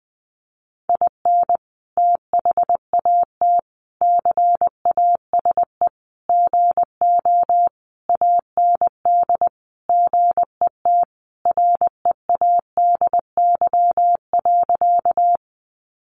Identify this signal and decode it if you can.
Morse code, heard as a keyed tone, about 20 wpm, 700 Hz. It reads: IN THAT CASE GO AND GET READY.
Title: Sentences from Top 700 Words 20wpm